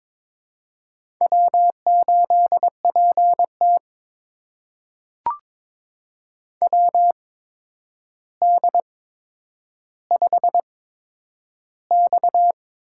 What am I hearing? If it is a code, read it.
Morse code, heard as a keyed tone, about 22 wpm, spaced as for 9 wpm, 700 Hz. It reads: W8PT WD5X
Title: USA Call Signs - 4 Characters - 8x Character Spacing - 22wpm